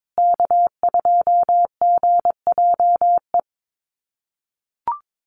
Morse code, heard as a keyed tone, about 22 wpm, 700 Hz. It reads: K2GJE
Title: USA Call Signs Repeated 2x 22wpm